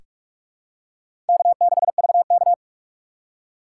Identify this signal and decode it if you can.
Morse code, heard as a keyed tone, about 47 wpm, 700 Hz. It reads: K6VK
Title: USA Call Signs - 3x Character Spacing - 45wpm